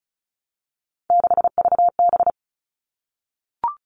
Morse code, heard as a keyed tone, about 34 wpm, 700 Hz. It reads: 6VB
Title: All Cut Numbers - 3-Digit 35wpm